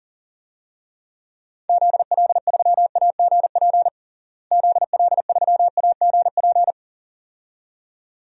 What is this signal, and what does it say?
Morse code, heard as a keyed tone, about 40 wpm, 700 Hz. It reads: ZL3AGP ZL3AGP
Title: DXCC-100 Call Signs - 6 Characters - Repeated 2x - 40wpm